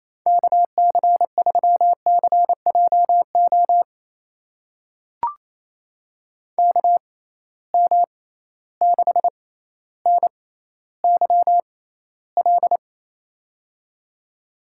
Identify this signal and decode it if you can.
Morse code, heard as a keyed tone, about 28 wpm, spaced as for 14 wpm, 700 Hz. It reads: KC3CJO KM6NYL
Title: USA Call Signs - 6 Characters - 6x Character Spacing - 28wpm